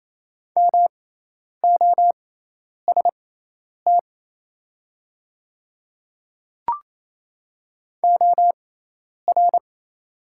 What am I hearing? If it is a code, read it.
Morse code, heard as a keyed tone, about 28 wpm, spaced as for 10 wpm, 690 Hz. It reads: MOST OR
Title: 6x Character Spacing - Top 200 Words - 28wpm